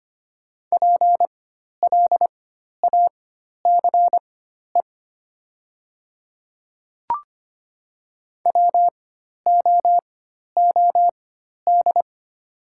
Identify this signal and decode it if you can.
Morse code, heard as a keyed tone, about 25 wpm, spaced as for 12 wpm, 700 Hz. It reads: PLACE WOOD
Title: 4x Character Spacing - Top 300 Words - 25wpm